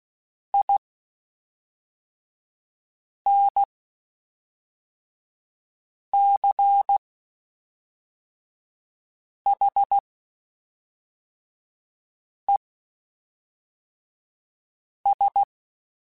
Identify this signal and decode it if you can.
Morse code, heard as a keyed tone, about 16 wpm, spaced as for 3 wpm, 800 Hz. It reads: INCHES
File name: (spaced) / QSOs Spaced-mp3s / WA7QOX 1 @5 sp